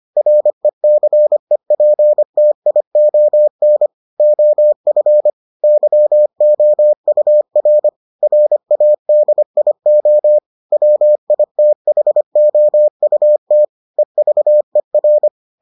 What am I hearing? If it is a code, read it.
Morse code, heard as a keyed tone, about 25 wpm, 600 Hz. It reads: RECEPTION OF YOUR RADIO WITHOUT EVER